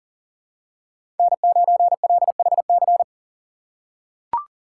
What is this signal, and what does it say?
Morse code, heard as a keyed tone, about 40 wpm, 700 Hz. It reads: N9LHC
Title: USA Call Signs - 6x Character Spacing - 40wpm